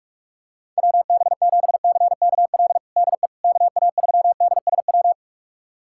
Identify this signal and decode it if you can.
Morse code, heard as a keyed tone, about 42 wpm, 700 Hz. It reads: WB7CKL DE KA3DSW
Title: USA Callsign DE Callsign - 6 Characters - 45wpm